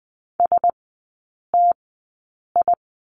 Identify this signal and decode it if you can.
Morse code, heard as a keyed tone, about 20 wpm, spaced as for 8 wpm, 700 Hz. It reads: STI
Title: 067 Lesson 12e - Letter d - Three Characters 20wpm